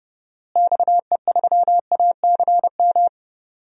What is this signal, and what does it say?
Morse code, heard as a keyed tone, about 32 wpm, 700 Hz. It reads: XE3ACM
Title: DXCC-100 Call Signs - 6 Characters - Repeated 3x - 30wpm